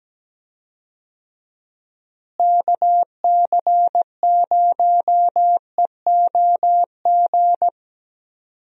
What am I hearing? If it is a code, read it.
Morse code, heard as a keyed tone, about 17 wpm, 700 Hz. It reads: KC0EOG
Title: USA Call Signs - 6 Characters - Repeated 3x - 17wpm